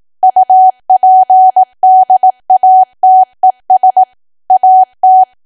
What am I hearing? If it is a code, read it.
Morse code, heard as a keyed tone, about 18 wpm, 750 Hz. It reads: UPDATES AT
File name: cwa / W1AW-Level2 / 18WPM